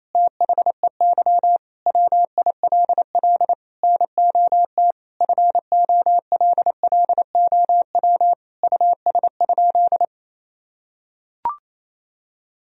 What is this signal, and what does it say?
Morse code, heard as a keyed tone, about 28 wpm, 700 Hz. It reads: THEY WILL NOT FOLLOW US?
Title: Sentences from Top 1000 Words - Review Part 1 - 28wpm